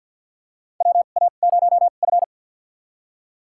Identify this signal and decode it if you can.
Morse code, heard as a keyed tone, about 52 wpm, spaced as for 36 wpm, 700 Hz. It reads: WA0F
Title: USA Call Signs - 4 Characters - 2x Character Spacing - 50wpm